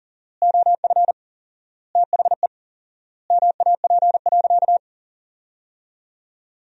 Morse code, heard as a keyed tone, about 40 wpm, 700 Hz. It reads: OF THE MAP.
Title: 4x Word-Spacing - Sentences from Top 400 Words - 40wpm